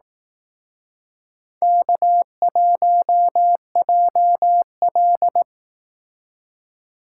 Morse code, heard as a keyed tone, about 18 wpm, 700 Hz. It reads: K1JL
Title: Speed Racing - USA 1x2 Call Signs 15wpm